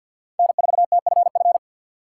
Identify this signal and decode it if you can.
Morse code, heard as a keyed tone, about 51 wpm, 700 Hz. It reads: N4TFF
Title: USA Call Signs - 3x Character Spacing - 50wpm